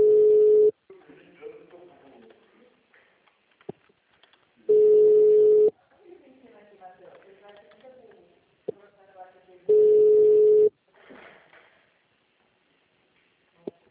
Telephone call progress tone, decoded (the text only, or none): ringback tone